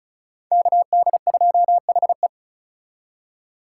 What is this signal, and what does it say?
Morse code, heard as a keyed tone, about 36 wpm, 700 Hz. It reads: KD2HE
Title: K1USN SST QSO Practice - 35wpm